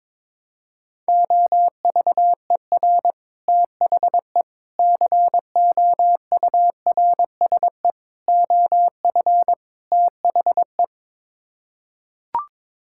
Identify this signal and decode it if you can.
Morse code, heard as a keyed tone, about 22 wpm, 700 Hz. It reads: OVER THE COURSE OF THE
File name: practice / 22wpm / Build A Phrase - 5-Word Phrases - 22wpm